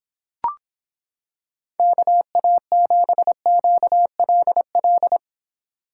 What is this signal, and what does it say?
Morse code, heard as a keyed tone, about 25 wpm, 700 Hz. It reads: KA7QLL